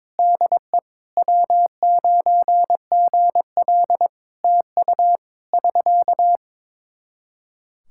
Morse code, heard as a keyed tone, about 22 wpm, 700 Hz. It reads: DE W9GL TU <SK>
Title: Practice QSO - Exchange 4 - Ending - 22wpm